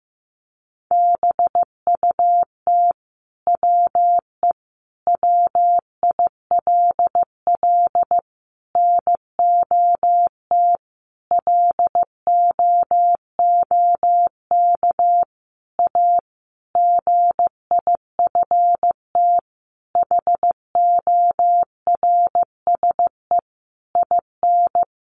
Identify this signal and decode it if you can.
Morse code, heard as a keyed tone, about 15 wpm, 700 Hz. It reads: BUT WE WILL NOT LOOK A GIFT HORSE IN
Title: Sentences from Top 1500 Words 15wpm